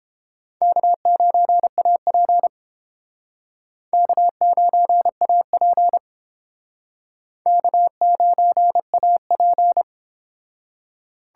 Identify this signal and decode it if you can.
Morse code, heard as a keyed tone, about 29 wpm, 700 Hz. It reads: K9AP K9AP K9AP